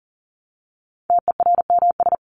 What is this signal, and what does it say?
Morse code, heard as a keyed tone, about 40 wpm, 700 Hz. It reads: TERMS
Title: ES:S Suffix - Top 100 Words with Base Words 40wpm